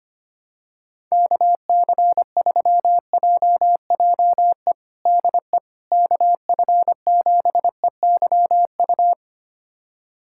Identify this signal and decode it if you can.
Morse code, heard as a keyed tone, about 25 wpm, 700 Hz. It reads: KC3JJE DE KF7EYU
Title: USA Callsign DE Callsign 25wpm